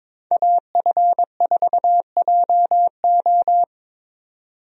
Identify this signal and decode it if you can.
Morse code, heard as a keyed tone, about 22 wpm, 700 Hz. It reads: AF4JO
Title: USA Call Signs - 5 Characters - Repeated 3x - 22wpm